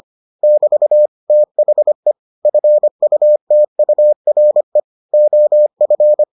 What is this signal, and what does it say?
Morse code, heard as a keyed tone, about 25 wpm, 600 Hz. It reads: <BT> THE FUTURE OF